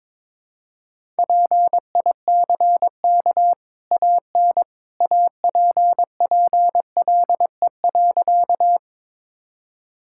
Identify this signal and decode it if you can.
Morse code, heard as a keyed tone, about 21 wpm, 690 Hz. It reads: PICK AN APPLE.
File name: practice / 22wpm / Easy Sentences with Hints - 22wpm